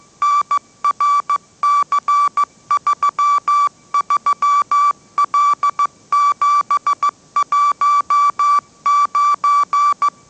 Morse code, heard as a keyed tone, about 15 wpm, 1160 Hz. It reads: NRC33L719